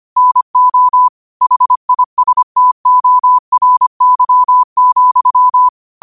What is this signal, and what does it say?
Morse code, heard as a keyed tone, about 25 wpm, 1000 Hz. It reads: NO HISTORY,
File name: qotdfiles / 25wpm